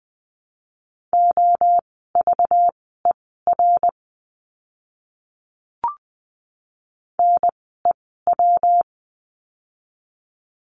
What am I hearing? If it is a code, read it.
Morse code, heard as a keyed tone, about 20 wpm, spaced as for 15 wpm, 700 Hz. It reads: OVER NEW
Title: 206 Lesson 33f - Letter j - Words 20wpm